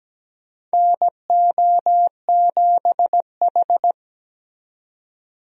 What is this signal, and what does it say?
Morse code, heard as a keyed tone, about 17 wpm, 700 Hz. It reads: NO7H
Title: USA Call Signs - 4 Characters - 17wpm